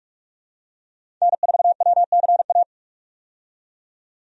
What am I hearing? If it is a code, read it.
Morse code, heard as a keyed tone, about 46 wpm, 700 Hz. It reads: N4WCA
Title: USA Call Signs - 5 Characters - 45wpm